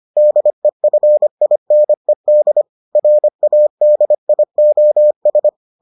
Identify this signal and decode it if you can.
Morse code, heard as a keyed tone, about 25 wpm, 600 Hz. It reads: DEFINED RADIOS